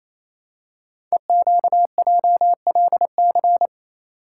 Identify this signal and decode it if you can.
Morse code, heard as a keyed tone, about 28 wpm, 700 Hz. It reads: EQJLC
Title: Code Groups - 5 Letter:Numbers - 28wpm